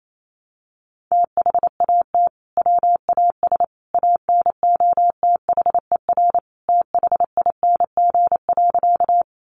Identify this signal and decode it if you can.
Morse code, heard as a keyed tone, about 28 wpm, 700 Hz. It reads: THAT WAS ANOTHER THING.
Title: Sentences from Top 600 Words 28wpm